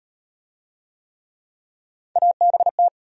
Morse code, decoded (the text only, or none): ABT